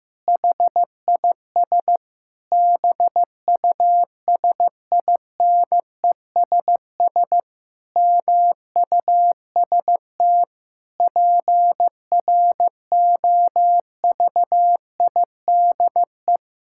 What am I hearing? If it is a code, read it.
Morse code, heard as a keyed tone, about 16 wpm, 700 Hz. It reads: HIS BUSINESS MUST PROVIDE
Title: Sentences from Top 1000 Words - Review Part 5 - 15wpm